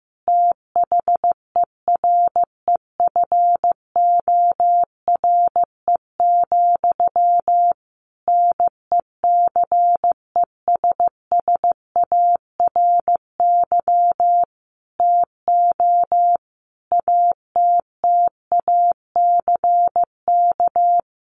Morse code, heard as a keyed tone, about 15 wpm, 700 Hz. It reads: THEREFORE, NECESSARY TO ATTACK
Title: Sentences from Top 1500 Words 15wpm